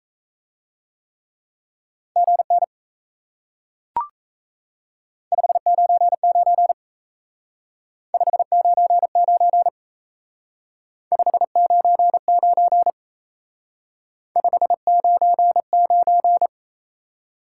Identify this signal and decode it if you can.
Morse code, heard as a keyed tone, about 37 wpm, 700 Hz. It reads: GN 599 599 599 599